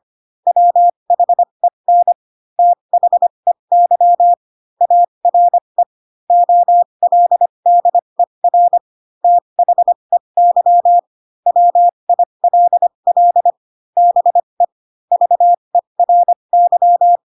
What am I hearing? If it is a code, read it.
Morse code, heard as a keyed tone, about 25 wpm, spaced as for 22 wpm, 700 Hz. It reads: WHEN THEY ARE OLDER THEY WILL BE VERY
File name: Pinocchio / Pinocchio ch4 18 25 ik1hge